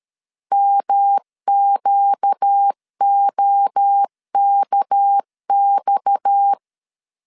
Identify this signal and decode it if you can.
Morse code, heard as a keyed tone, about 13 wpm, 800 Hz. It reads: MQOKX
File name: LEZIONI / 6 PKX0 / 6 PKX 12wpm 2024